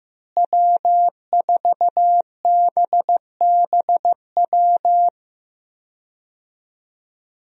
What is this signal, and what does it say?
Morse code, heard as a keyed tone, about 15 wpm, 700 Hz. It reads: W4BBW
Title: USA Call Signs - 5 Characters - Repeated 3x - 15wpm